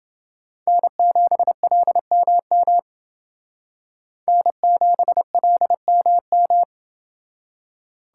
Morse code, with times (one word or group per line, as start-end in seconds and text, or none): 0.7-2.8 N7LMM
4.3-6.6 N7LMM